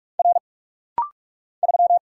Morse code, heard as a keyed tone, about 45 wpm, 700 Hz. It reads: R 3
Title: Single Letter-Number Alt - Rapid-Fire 45wpm